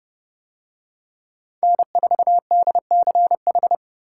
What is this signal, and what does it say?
Morse code, heard as a keyed tone, about 28 wpm, 700 Hz. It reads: N4DCH